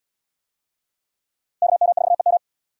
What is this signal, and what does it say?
Morse code, heard as a keyed tone, about 78 wpm, 690 Hz. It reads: KM4EM